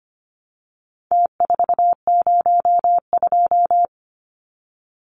Morse code, heard as a keyed tone, about 24 wpm, 700 Hz. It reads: T402
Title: Numbers - Contest Style - 4-Digit 25wpm